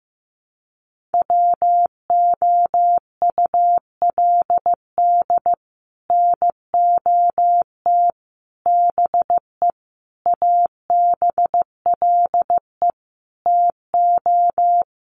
Morse code, heard as a keyed tone, about 15 wpm, 700 Hz. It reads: WOULD NOT BE ABLE TO
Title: Sets of 5 Words 15wpm